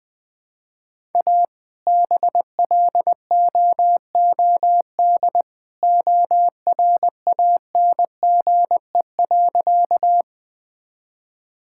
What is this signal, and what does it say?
Morse code, heard as a keyed tone, about 20 wpm, 700 Hz. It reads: A BLOOD ORANGE.
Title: All-Age Appropriate Jokes - 20wpm